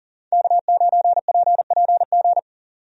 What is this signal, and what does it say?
Morse code, heard as a keyed tone, about 38 wpm, 700 Hz. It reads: K9PPG
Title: USA Call Signs - 5 Characters - 40wpm